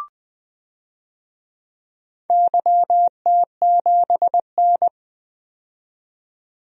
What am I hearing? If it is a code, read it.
Morse code, heard as a keyed tone, about 20 wpm, 700 Hz. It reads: YT7N
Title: DXCC-100 Call Signs - 4 Characters - 20wpm